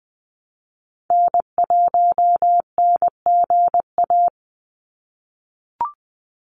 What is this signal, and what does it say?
Morse code, heard as a keyed tone, about 20 wpm, 700 Hz. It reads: N1NGA
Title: USA Call Signs 20wpm